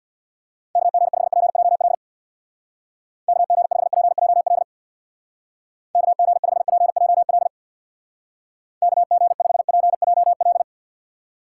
Morse code, heard as a keyed tone, about 66 wpm, 700 Hz. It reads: KG5PJL KG5PJL KG5PJL KG5PJL